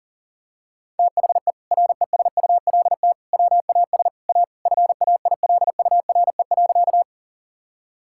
Morse code, heard as a keyed tone, about 40 wpm, 700 Hz. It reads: THE RESULT WAS A FAILURE.
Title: Sentences from Top 1000 Words - Review Part 6 - 40wpm